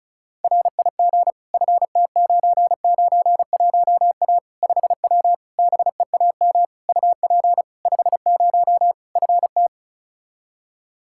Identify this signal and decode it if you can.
Morse code, heard as a keyed tone, about 36 wpm, 700 Hz. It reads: RIG FT991A 5W BEAM UP 50 FT
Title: Practice QSO - Exchange 2 - Info - 35wpm